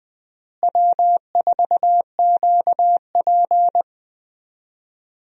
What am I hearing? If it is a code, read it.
Morse code, heard as a keyed tone, about 20 wpm, 700 Hz. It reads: W4QP
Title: USA Call Signs - 4 Characters - Repeated 3x - 20wpm